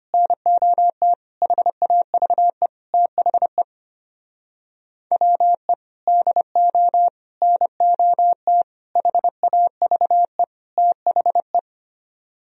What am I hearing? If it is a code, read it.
Morse code, heard as a keyed tone, about 33 wpm, 700 Hz. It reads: NOT HAVE THE WE DO NOT HAVE THE